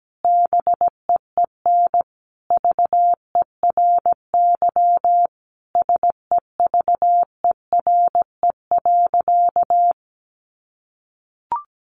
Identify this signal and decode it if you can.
Morse code, heard as a keyed tone, about 17 wpm, 700 Hz. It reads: BEEN VERY SEVERE.
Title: Sentences from Top 1400 Words 17wpm